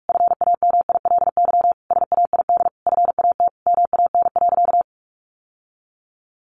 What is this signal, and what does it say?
Morse code, heard as a keyed tone, about 44 wpm, 700 Hz. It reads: FAMILY SAID FAT MAN.
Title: Sentences from Top 1100 Words 45wpm